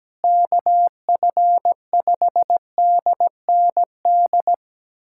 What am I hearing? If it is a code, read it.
Morse code, heard as a keyed tone, about 17 wpm, 700 Hz. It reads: KF5DND
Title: USA Callsign DE Callsign 17wpm